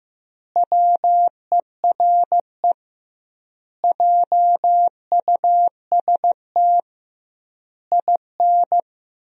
Morse code, heard as a keyed tone, about 16 wpm, 700 Hz. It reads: WERE JUST IN
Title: 2x Word-Spacing - Sentences from Top 300 Words - 15wpm